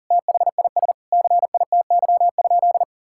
Morse code, heard as a keyed tone, about 40 wpm, 700 Hz. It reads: THIS CITY?
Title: Sentences from Top 1600 Words 40wpm